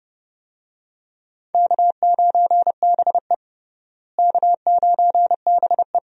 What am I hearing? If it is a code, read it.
Morse code, heard as a keyed tone, about 30 wpm, 700 Hz. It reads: K9BE K9BE
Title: USA Call Signs - 4 Characters - Repeated 3x - 30wpm